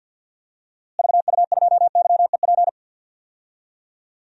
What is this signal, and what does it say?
Morse code, heard as a keyed tone, about 50 wpm, 700 Hz. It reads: VU2YEP
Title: DXCC-100 Call Signs - 6 Characters - Repeated 2x - 50wpm